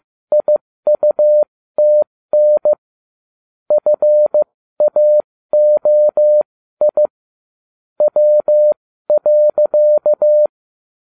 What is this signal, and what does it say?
Morse code, heard as a keyed tone, about 15 wpm, spaced as for 13 wpm, 600 Hz. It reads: IUTN FAOI W.